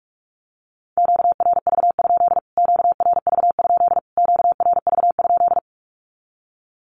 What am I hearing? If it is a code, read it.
Morse code, heard as a keyed tone, about 44 wpm, 700 Hz. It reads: QRV? QRV? QRV?
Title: Q-Codes Repeated 3x 45wpm